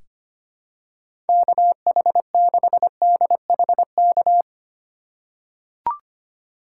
Morse code, heard as a keyed tone, about 25 wpm, 700 Hz. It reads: KH6DHK
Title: K1USN SST QSO Practice - 25wpm